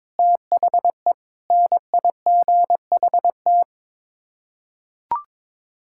Morse code, heard as a keyed tone, about 21 wpm, 700 Hz. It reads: THE NIGHT
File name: practice / 22wpm / Sets of 7 Words - 22wpm